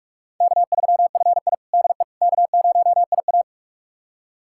Morse code, heard as a keyed tone, about 46 wpm, 700 Hz. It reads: K3FI DE K0IA
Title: USA Callsign DE Callsign - 4 Characters - 45wpm